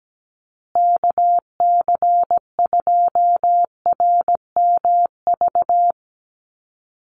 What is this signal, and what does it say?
Morse code, heard as a keyed tone, about 17 wpm, 700 Hz. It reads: KC2RMV
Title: USA Call Signs Repeated 3x 17wpm